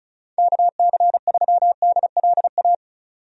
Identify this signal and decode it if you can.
Morse code, heard as a keyed tone, about 34 wpm, 700 Hz. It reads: KC3DLA
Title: USA Call Signs - 6 Characters - 35wpm